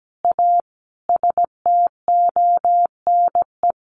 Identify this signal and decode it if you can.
Morse code, heard as a keyed tone, about 17 wpm, 700 Hz. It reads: A STONE
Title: A Sound Of Thunder 17wpm